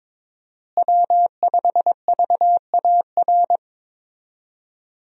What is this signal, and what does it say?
Morse code, heard as a keyed tone, about 23 wpm, 700 Hz. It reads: W5VAR
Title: USA Call Signs - 5 Characters - Repeated 3x - 22wpm